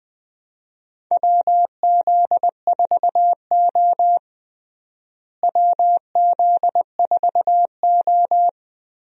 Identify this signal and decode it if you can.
Morse code, heard as a keyed tone, about 20 wpm, 700 Hz. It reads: WZ4O WZ4O
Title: USA Call Signs - 4 Characters - Repeated 2x - 20wpm